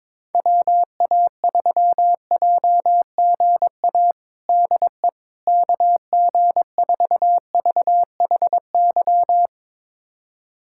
Morse code, heard as a keyed tone, about 21 wpm, 700 Hz. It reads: WA3JGA DE KG4VHY
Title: USA Callsign DE Callsign 22wpm